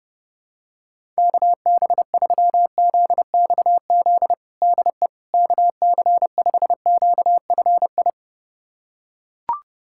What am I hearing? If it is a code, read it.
Morse code, heard as a keyed tone, about 30 wpm, 700 Hz. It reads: KB3ZXZ DE KC5QFI
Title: USA Callsign DE Callsign - 6 Characters - 30wpm